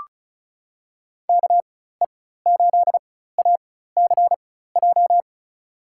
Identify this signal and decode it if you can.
Morse code, heard as a keyed tone, about 34 wpm, spaced as for 16 wpm, 700 Hz. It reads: KE8ACJ
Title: USA Call Signs - 6 Characters - 4x Character Spacing - 35wpm